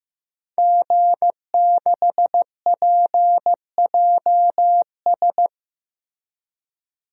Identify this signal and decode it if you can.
Morse code, heard as a keyed tone, about 16 wpm, 700 Hz. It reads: G6PJS
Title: DXCC-100 Call Signs - 5 Characters - Repeated 2x - 15wpm